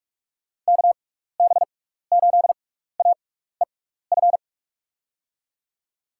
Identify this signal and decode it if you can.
Morse code, heard as a keyed tone, about 41 wpm, spaced as for 15 wpm, 700 Hz. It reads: KB8AEF
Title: USA Call Signs - 6 Characters - 6x Character Spacing - 45wpm